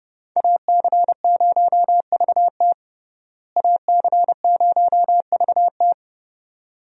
Morse code, heard as a keyed tone, about 30 wpm, 700 Hz. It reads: AC0VT AC0VT
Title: USA Call Signs - 5 Characters - Repeated 2x - 30wpm